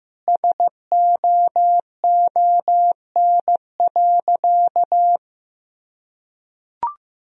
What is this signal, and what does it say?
Morse code, heard as a keyed tone, about 16 wpm, 700 Hz. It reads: SOON.
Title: Sentences from Top 1000 Words - Review Part 4 - 15wpm